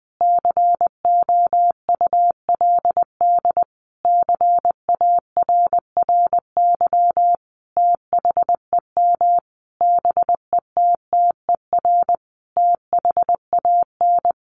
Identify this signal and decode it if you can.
Morse code, heard as a keyed tone, about 20 wpm, 700 Hz. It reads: COULD CARRY THEM BETTER THAN
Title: Sentences from Top 1200 Words 20wpm